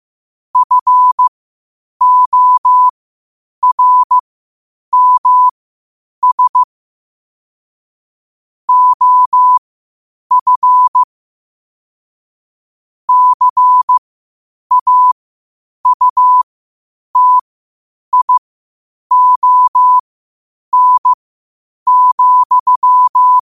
Morse code, heard as a keyed tone, about 15 wpm, spaced as for 8 wpm, 1000 Hz. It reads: FORMS OF CAUTION,
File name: qotdfiles / 8wpm